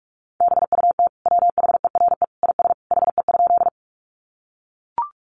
Morse code, heard as a keyed tone, about 44 wpm, 700 Hz. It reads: BUT WHERE IS HE?
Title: Sentences from Top 200 Words 45wpm